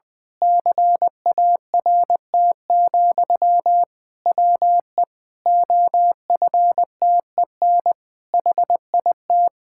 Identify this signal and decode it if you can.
Morse code, heard as a keyed tone, about 19 wpm, 700 Hz. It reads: CART, WE OFTEN HIT
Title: Pangram Challenge v2 - 20wpm